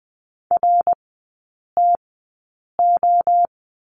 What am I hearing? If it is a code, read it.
Morse code, heard as a keyed tone, about 20 wpm, spaced as for 8 wpm, 700 Hz. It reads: RTO
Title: 107 Lesson 18e - Letter m - Three Characters 20wpm